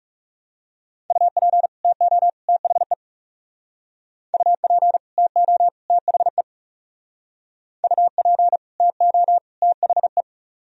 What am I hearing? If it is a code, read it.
Morse code, heard as a keyed tone, about 40 wpm, 700 Hz. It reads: UP TO THE UP TO THE UP TO THE